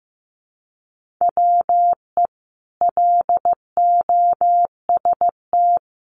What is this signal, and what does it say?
Morse code, heard as a keyed tone, about 15 wpm, 700 Hz. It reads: WE LOST
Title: Sentences from Top 1500 Words 15wpm